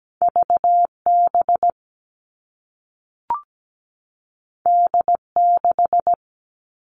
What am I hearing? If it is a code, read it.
Morse code, heard as a keyed tone, about 17 wpm, 700 Hz. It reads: VB D6
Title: All Cut Numbers - 2-Digit 17wpm